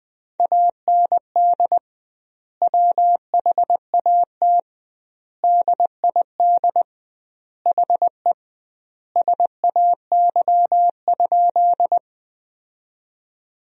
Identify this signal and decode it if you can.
Morse code, heard as a keyed tone, about 21 wpm, 700 Hz. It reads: AND WHAT DID HE SAY?
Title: 2x Word-Spacing - Sentences from Top 200 Words - 20wpm